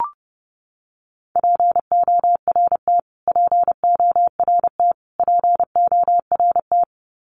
Morse code, heard as a keyed tone, about 30 wpm, 700 Hz. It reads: PORT PORT PORT
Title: Top 300 Words Repeated 30wpm